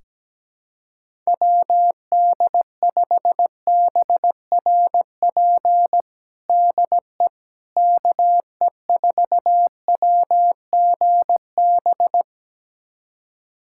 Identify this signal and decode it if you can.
Morse code, heard as a keyed tone, about 17 wpm, 700 Hz. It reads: WD5BRP DE KE4WGB
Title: USA Callsign DE Callsign 17wpm